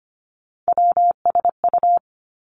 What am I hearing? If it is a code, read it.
Morse code, heard as a keyed tone, about 25 wpm, 700 Hz. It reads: WSU
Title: Three Challenging Letter-Number Combinations 25wpm